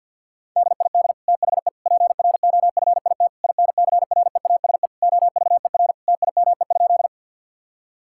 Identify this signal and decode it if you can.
Morse code, heard as a keyed tone, about 51 wpm, 700 Hz. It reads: DID THE PROFIT INCREASE OVER TIME?
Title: Sentences from Top 1000 Words - Review Part 5 - 50wpm